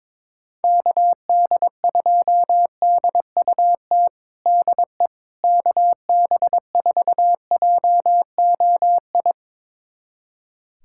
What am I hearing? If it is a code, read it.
Morse code, heard as a keyed tone, about 22 wpm, 700 Hz. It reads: KD2DUT DE KB4JOI
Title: USA Callsign DE Callsign - 6 Characters - 22wpm